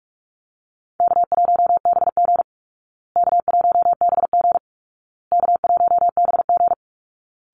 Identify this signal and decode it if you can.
Morse code, heard as a keyed tone, about 44 wpm, 700 Hz. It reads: K1BG K1BG K1BG
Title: Easy Call Signs Repeated 3x - 45wpm